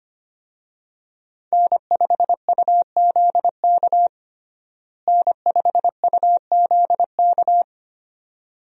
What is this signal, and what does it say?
Morse code, heard as a keyed tone, about 25 wpm, 700 Hz. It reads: N5UZK N5UZK
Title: USA Call Signs - 5 Characters - Repeated 2x - 25wpm